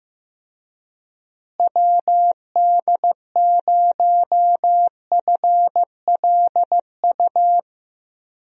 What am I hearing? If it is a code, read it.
Morse code, heard as a keyed tone, about 15 wpm, 700 Hz. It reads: WD0FLU
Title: USA Call Signs - 6 Characters - Repeated 3x - 15wpm